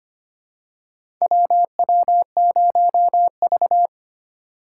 Morse code, heard as a keyed tone, about 25 wpm, 700 Hz. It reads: WW0V
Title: USA Call Signs - 2x Character Spacing - 25wpm